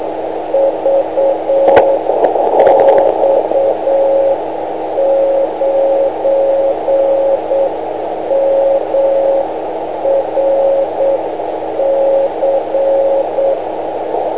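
Morse code, heard as a keyed tone, about 8 wpm, 550 Hz, with radio noise. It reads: VX9MRC
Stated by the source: Morse from a 500 kHz transmitter.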